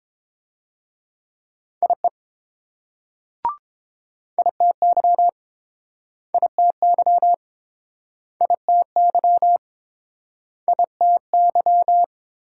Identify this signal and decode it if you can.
Morse code, heard as a keyed tone, about 30 wpm, 700 Hz. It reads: IE ITY ITY ITY ITY